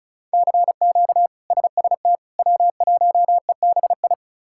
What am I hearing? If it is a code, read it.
Morse code, heard as a keyed tone, about 34 wpm, 700 Hz. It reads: CQ SST W1EBI
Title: K1USN SST QSO Practice - 35wpm